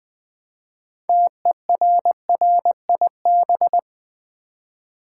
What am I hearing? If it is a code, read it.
Morse code, heard as a keyed tone, about 20 wpm, 700 Hz. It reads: TERRIB
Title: Build A Word - 2 Letters At A Time - 8-Letter Words - 20wpm